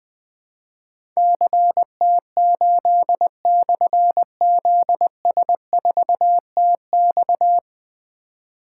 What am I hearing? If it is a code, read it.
Morse code, heard as a keyed tone, about 20 wpm, 700 Hz. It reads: CT8/ZS4TX
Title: Difficult Call Signs 20wpm